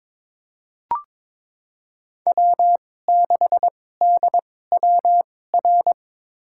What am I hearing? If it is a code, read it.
Morse code, heard as a keyed tone, about 22 wpm, spaced as for 16 wpm, 700 Hz. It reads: W6DWR
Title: USA Call Signs - 5 Characters - 2x Character Spacing - 22wpm